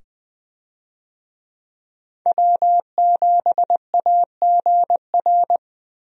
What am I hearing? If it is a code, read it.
Morse code, heard as a keyed tone, about 21 wpm, 700 Hz. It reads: W7AGR
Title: USA Call Signs - 5 Characters - 20wpm